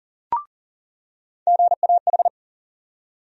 Morse code, heard as a keyed tone, about 40 wpm, 700 Hz. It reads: GAH
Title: USA Call Sign Postfixes - 3 Letters 40wpm